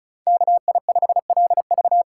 Morse code, heard as a keyed tone, about 47 wpm, 700 Hz. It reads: KI5LV